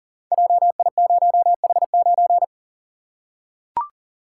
Morse code, heard as a keyed tone, about 40 wpm, 700 Hz. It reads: JI0H9
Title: Code Groups - 5 Letter:Numbers - 40wpm